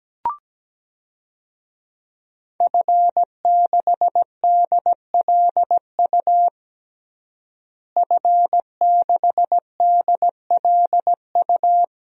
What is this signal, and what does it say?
Morse code, heard as a keyed tone, about 17 wpm, 700 Hz. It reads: F6DLU F6DLU
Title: DXCC-100 Call Signs - 5 Characters - Repeated 2x - 17wpm